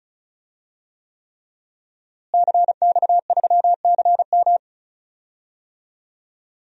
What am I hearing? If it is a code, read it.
Morse code, heard as a keyed tone, about 36 wpm, 700 Hz. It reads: CX3CM
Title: DXCC-100 Call Signs - 5 Characters - Repeated 3x - 35wpm